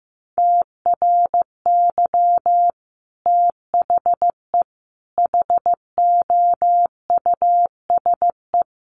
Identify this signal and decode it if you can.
Morse code, heard as a keyed tone, about 15 wpm, 700 Hz. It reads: TRY THE HOUSE
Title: Sentences from Top 1300 Words 15wpm